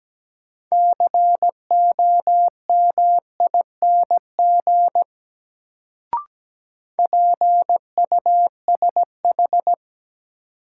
Morse code, heard as a keyed tone, about 17 wpm, 700 Hz. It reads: COMING PUSH
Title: Sentences from Top 1300 Words - New Word Drill - 17wpm